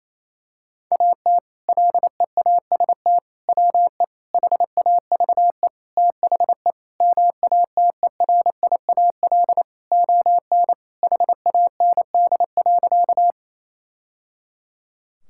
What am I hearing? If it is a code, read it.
Morse code, heard as a keyed tone, about 28 wpm, 700 Hz. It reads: AT LEAST WE HAVE THE MATERIAL ON HAND.
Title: Sentences from Top 1000 Words - Review Part 5 - 28wpm